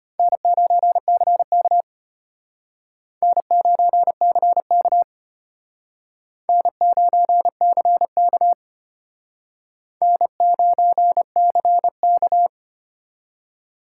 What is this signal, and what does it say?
Morse code, heard as a keyed tone, about 31 wpm, 700 Hz. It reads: N9CK N9CK N9CK N9CK